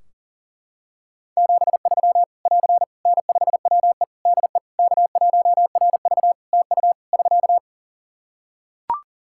Morse code, heard as a keyed tone, about 40 wpm, 700 Hz. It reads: 73 <AR> N5WE DE K1RV TU <SK>
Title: Practice QSO - Exchange 4 - Ending - 40wpm